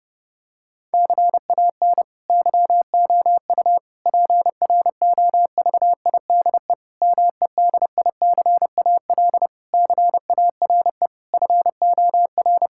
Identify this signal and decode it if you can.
Morse code, heard as a keyed tone, about 30 wpm, 700 Hz. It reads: CAN YOU PROVIDE MEDICAL CARE FOR
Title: Sentences from Top 1000 Words - Review Part 5 - 30wpm